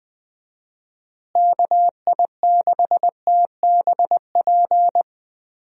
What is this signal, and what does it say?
Morse code, heard as a keyed tone, about 21 wpm, 700 Hz. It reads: KI6TBP
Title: USA Call Signs - 6 Characters - Repeated 2x - 20wpm